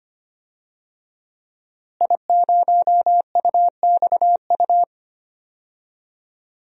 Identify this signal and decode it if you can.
Morse code, heard as a keyed tone, about 25 wpm, 700 Hz. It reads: I0UXU
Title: DXCC-100 Call Signs - 5 Characters - Repeated 2x - 25wpm